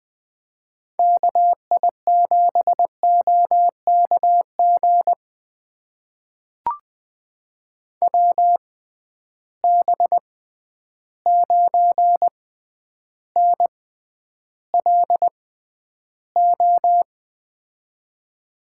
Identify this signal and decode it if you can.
Morse code, heard as a keyed tone, about 20 wpm, spaced as for 10 wpm, 700 Hz. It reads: KI7OKG WB9NLO
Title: USA Call Signs - 6 Characters - 6x Character Spacing - 20wpm